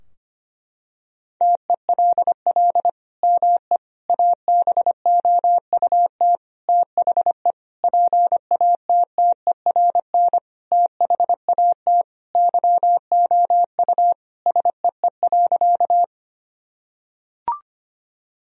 Morse code, heard as a keyed tone, about 25 wpm, 700 Hz. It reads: TELL ME ABOUT THE PATTERN THAT YOU SEE.
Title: Sentences from Top 1000 Words - Review Part 2 - 25wpm